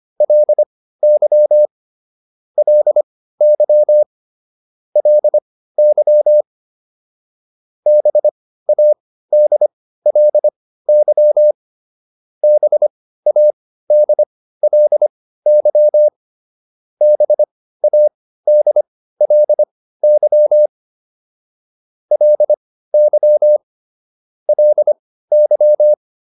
Morse code, heard as a keyed tone, about 25 wpm, spaced as for 15 wpm, 600 Hz. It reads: LY LY LY BADLY BADLY BADLY LY LY